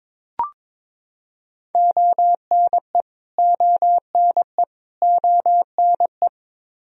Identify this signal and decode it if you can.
Morse code, heard as a keyed tone, about 22 wpm, 700 Hz. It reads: ONE ONE ONE
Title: Top 100 Words Repeated 22wpm